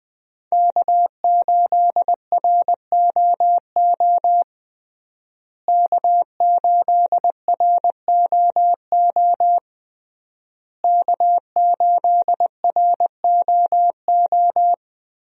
Morse code, heard as a keyed tone, about 20 wpm, 700 Hz. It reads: K8ROO K8ROO K8ROO
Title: USA Call Signs - 5 Characters - Repeated 3x - 20wpm